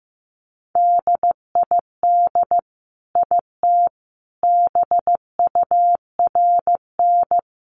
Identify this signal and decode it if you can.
Morse code, heard as a keyed tone, about 15 wpm, 700 Hz. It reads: DID IT BURN
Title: Sentences from Top 1300 Words 15wpm